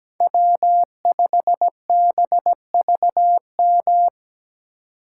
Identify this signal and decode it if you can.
Morse code, heard as a keyed tone, about 20 wpm, 700 Hz. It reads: W5BVM
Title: Speed Racing - USA 1x3 Call Signs 17wpm